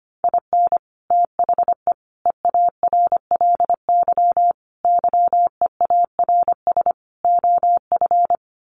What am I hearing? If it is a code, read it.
Morse code, heard as a keyed tone, about 25 wpm, 690 Hz. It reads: IN THE EARLY YEARS OF
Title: Sets of 5 Words Encore 25wpm